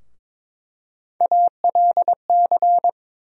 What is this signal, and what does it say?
Morse code, heard as a keyed tone, about 22 wpm, 700 Hz. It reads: ALC
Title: USA Call Sign Postfixes - 3 Letters 22wpm